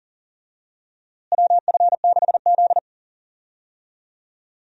Morse code, heard as a keyed tone, about 40 wpm, 700 Hz. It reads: WF6Z
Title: USA Call Signs - 4 Characters - Repeated 3x - 40wpm